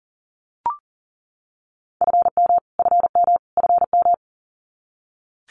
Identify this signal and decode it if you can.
Morse code, heard as a keyed tone, about 40 wpm, 700 Hz. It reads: FM FM FM
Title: Abbreviations Repeated 3x - 40wpm